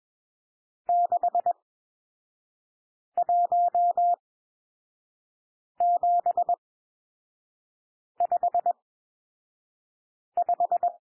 Morse code, heard as a keyed tone, about 21 wpm, spaced as for 5 wpm, 700 Hz. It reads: 61755
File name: cwmp3 / N1 SOUND NUMBERS 21 wpm